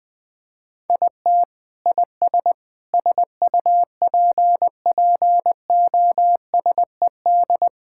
Morse code, heard as a keyed tone, about 20 wpm, 700 Hz. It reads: IT IS SUPPOSED
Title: Sentences with 3-Word Phrases - Spoken Beginning - 20wpm